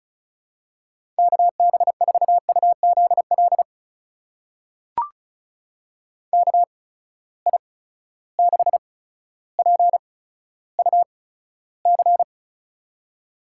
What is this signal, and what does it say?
Morse code, heard as a keyed tone, about 36 wpm, spaced as for 15 wpm, 700 Hz. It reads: KB4UZL KI6PUC
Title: USA Call Signs - 6 Characters - 8x Character Spacing - 35wpm